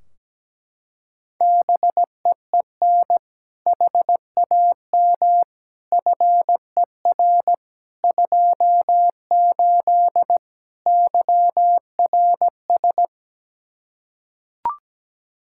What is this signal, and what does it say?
Morse code, heard as a keyed tone, about 17 wpm, 700 Hz. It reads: BEEN HAM FER 28 YRS
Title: K1USN SST QSO Practice 17wpm